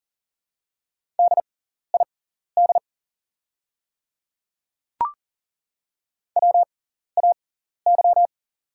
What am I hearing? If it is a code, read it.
Morse code, heard as a keyed tone, about 40 wpm, spaced as for 14 wpm, 700 Hz. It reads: DID WAY
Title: 6x Character Spacing - Top 200 Words - 40wpm